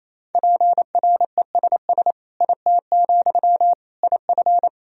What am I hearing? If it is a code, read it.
Morse code, heard as a keyed tone, about 28 wpm, 700 Hz. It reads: PRESS IT, IF
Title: Sentences from Top 1000 Words - Review Part 1 - 28wpm